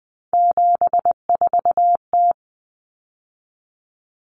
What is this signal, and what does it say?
Morse code, heard as a keyed tone, about 20 wpm, 700 Hz. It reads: 74T
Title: Common Cut Numbers - 3-Digit 20wpm